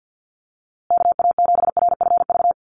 Morse code, heard as a keyed tone, about 50 wpm, 700 Hz. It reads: KA7RFV
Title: USA Call Signs Repeated 2x 50wpm